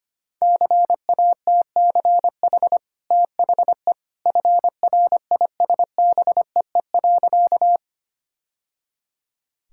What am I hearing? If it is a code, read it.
Morse code, heard as a keyed tone, about 25 wpm, 690 Hz. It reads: CATCH THE FRISBEE.
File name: Easy Sentences with Hints - 25wpm